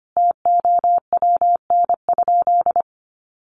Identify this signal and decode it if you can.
Morse code, heard as a keyed tone, about 25 wpm, 700 Hz. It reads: TOWN?
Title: Sentences from Top 1200 Words 25wpm